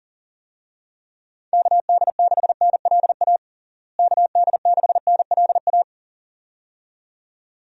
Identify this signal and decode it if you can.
Morse code, heard as a keyed tone, about 40 wpm, 700 Hz. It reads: KD6NLA KD6NLA
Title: USA Call Signs - 6 Characters - Repeated 2x - 40wpm